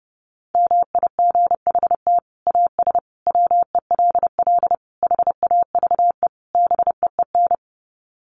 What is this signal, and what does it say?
Morse code, heard as a keyed tone, about 30 wpm, 700 Hz. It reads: MIGHT AS WELL HAVE BEEN
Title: Sets of 5 Words Encore2 30wpm